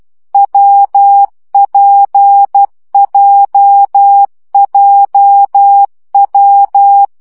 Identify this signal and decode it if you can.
Morse code, heard as a keyed tone, about 12 wpm, 800 Hz. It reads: WPJJW
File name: CW gyakorlas w p j 10 baud